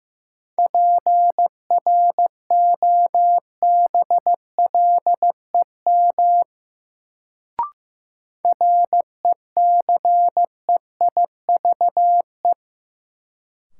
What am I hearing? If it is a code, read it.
Morse code, heard as a keyed tone, about 16 wpm, 700 Hz. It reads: PROBLEM RECEIVE
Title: Sentences from Top 1000 Words - New Word Drill - 15wpm